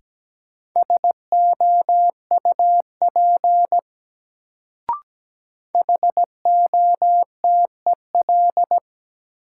Morse code, heard as a keyed tone, about 17 wpm, 700 Hz. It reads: SOUP HOTEL
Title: Sentences from Top 1600 Words - New Word Drill Alt - 17wpm